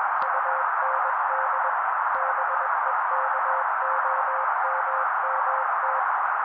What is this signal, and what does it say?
Morse code, heard as a keyed tone, about 21 wpm, 530 Hz. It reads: UND BEKOMMT
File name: daily / 20wpm noise